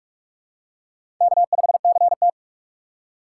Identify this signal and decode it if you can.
Morse code, heard as a keyed tone, about 44 wpm, 690 Hz. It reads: K5CT
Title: USA Call Signs - 4 Characters - 3x Character Spacing - 45wpm